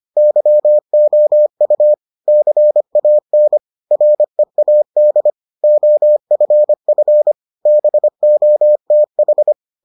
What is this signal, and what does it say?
Morse code, heard as a keyed tone, about 25 wpm, 600 Hz. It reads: YOU CAN READ OFF BOTH